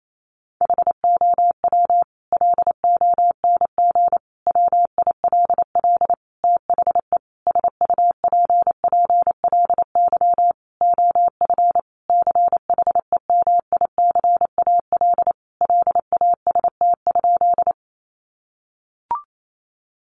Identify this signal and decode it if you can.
Morse code, heard as a keyed tone, about 28 wpm, 700 Hz. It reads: HOW LONG WILL THE SUPPLY OF CHEMICAL LAST?
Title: Sentences from Top 1200 Words 28wpm